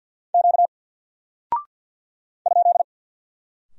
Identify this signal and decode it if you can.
Morse code, heard as a keyed tone, about 48 wpm, 700 Hz. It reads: Q ?
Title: Single Character (Letter:Number:Punctuation:Prosign) 50wpm